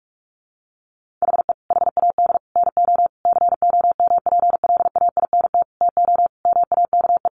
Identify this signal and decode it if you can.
Morse code, heard as a keyed tone, about 44 wpm, 700 Hz. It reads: HE HAD NO COMPLAINT TO MAKE
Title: Sentences from Top 1400 Words 45wpm